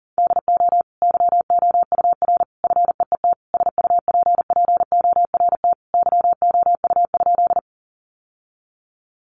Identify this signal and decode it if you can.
Morse code, heard as a keyed tone, about 37 wpm, 700 Hz. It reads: DO YOUR FEET SUPPORT YOU?
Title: Sentences from Top 800 Words 40wpm